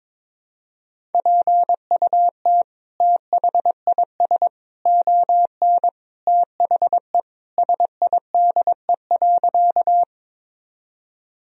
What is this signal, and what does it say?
Morse code, heard as a keyed tone, about 22 wpm, 700 Hz. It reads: PUT THIS ON THE SIDE.
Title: Sentences from Top 1000 Words - Review Part 1 - 22wpm